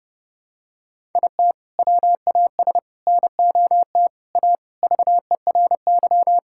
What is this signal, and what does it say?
Morse code, heard as a keyed tone, about 32 wpm, 700 Hz. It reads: IT WAS NOT A VERY
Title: Sentences from Top 1000 Words - Review Part 6 - 30wpm